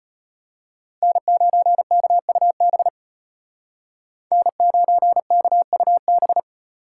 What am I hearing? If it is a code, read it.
Morse code, heard as a keyed tone, about 35 wpm, 700 Hz. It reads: N9KUB N9KUB